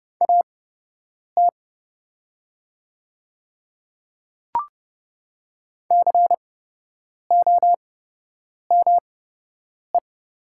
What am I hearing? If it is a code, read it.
Morse code, heard as a keyed tone, about 30 wpm, spaced as for 8 wpm, 700 Hz. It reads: AT COME
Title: 8x Character Spacing - Top 500 Words - 30wpm